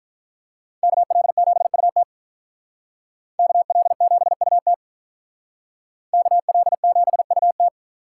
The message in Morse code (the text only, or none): KL7UT KL7UT KL7UT